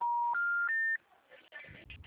Special information tone.